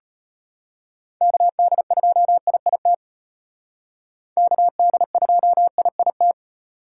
Morse code, heard as a keyed tone, about 33 wpm, 700 Hz. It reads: KD2IIT KD2IIT